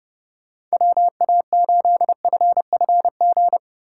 Morse code, heard as a keyed tone, about 38 wpm, 700 Hz. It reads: WA8FFG